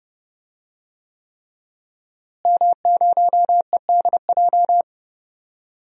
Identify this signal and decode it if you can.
Morse code, heard as a keyed tone, about 31 wpm, 700 Hz. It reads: M0EDJ
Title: DXCC-100 Call Signs - 5 Characters - Repeated 3x - 30wpm